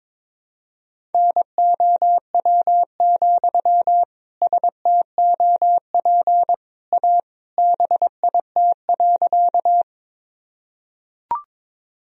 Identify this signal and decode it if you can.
Morse code, heard as a keyed tone, about 22 wpm, 700 Hz. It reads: NOW, STOP A BIT.
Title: Sentences from Top 1000 Words - Review Part 1 - 22wpm